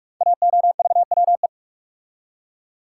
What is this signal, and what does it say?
Morse code, heard as a keyed tone, about 44 wpm, 700 Hz. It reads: AOVWE
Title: Code Groups - 5 Letters - 45wpm